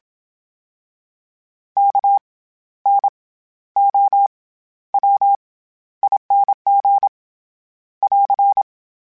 Morse code, heard as a keyed tone, about 27 wpm, 800 Hz. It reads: K N O W ING <AR>
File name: Special INGs-mp3s / Special INGs-mp3s / SlowToFaster ing @25